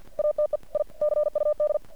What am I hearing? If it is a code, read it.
Morse code, heard as a keyed tone, about 52 wpm, 600 Hz. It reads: ATE I CUD
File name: archived / CRSnet